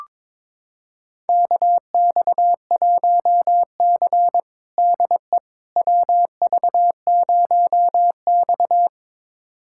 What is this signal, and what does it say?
Morse code, heard as a keyed tone, about 22 wpm, 700 Hz. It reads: KX1C DE WV0X
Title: USA Callsign DE Callsign - 4 Characters - 22wpm